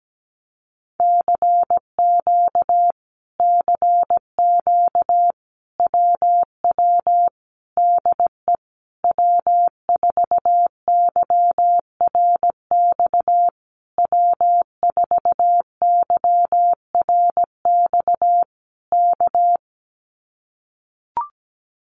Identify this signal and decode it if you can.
Morse code, heard as a keyed tone, about 17 wpm, 700 Hz. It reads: CQ CQ WW DE W4YRX W4YRX K
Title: Common QSO Phrases Encore 17wpm